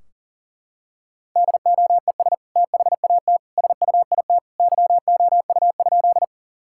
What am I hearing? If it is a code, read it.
Morse code, heard as a keyed tone, about 37 wpm, 700 Hz. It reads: DOES THAT SUIT YOU?
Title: Sentences from Top 1600 Words 40wpm